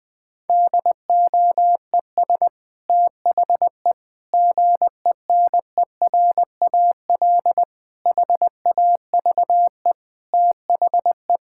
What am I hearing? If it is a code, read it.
Morse code, heard as a keyed tone, about 21 wpm, 700 Hz. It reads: DOES THE GENERAL HAVE THE
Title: Sentences from Top 1000 Words - Review Part 6 - 20wpm